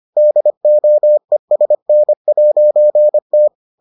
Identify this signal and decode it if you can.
Morse code, heard as a keyed tone, about 25 wpm, 600 Hz. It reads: DOESN'T